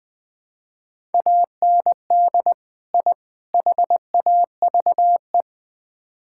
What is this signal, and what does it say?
Morse code, heard as a keyed tone, about 20 wpm, 700 Hz. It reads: AND I HAVE
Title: Build A Phrase - 3-Word Phrases - 20wpm